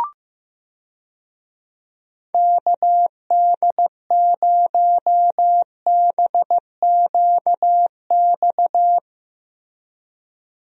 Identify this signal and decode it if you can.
Morse code, heard as a keyed tone, about 15 wpm, 700 Hz. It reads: KD0BQX
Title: USA Call Signs - 6 Characters - 15wpm